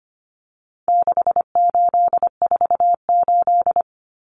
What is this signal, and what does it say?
Morse code, heard as a keyed tone, about 25 wpm, 700 Hz. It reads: 6848
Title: Numbers - 4-Digit 25wpm